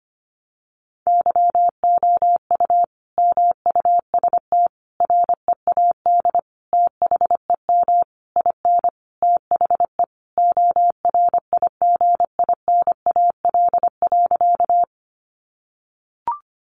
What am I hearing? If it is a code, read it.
Morse code, heard as a keyed tone, about 25 wpm, 700 Hz. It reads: YOU MUST READ THEM IN THE ORIGINAL.
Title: Sentences from Top 900 Words 25wpm